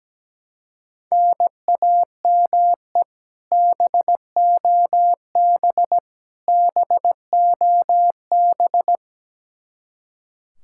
Morse code, heard as a keyed tone, about 17 wpm, 700 Hz. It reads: NAME BOB BOB
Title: K1USN SST QSO Practice 17wpm